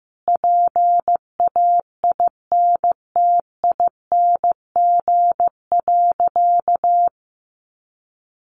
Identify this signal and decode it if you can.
Morse code, heard as a keyed tone, about 15 wpm, 700 Hz. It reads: PAINTING.
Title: Sentences from Top 1200 Words 15wpm